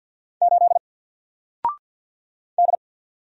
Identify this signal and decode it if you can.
Morse code, heard as a keyed tone, about 50 wpm, 700 Hz. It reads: 8 D
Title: Single Letter-Number 50wpm